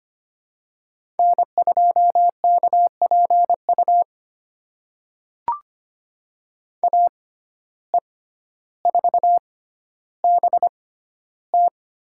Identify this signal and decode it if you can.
Morse code, heard as a keyed tone, about 25 wpm, spaced as for 13 wpm, 700 Hz. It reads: N2KPU AE4BT
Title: USA Call Signs - 5 Characters - 6x Character Spacing - 25wpm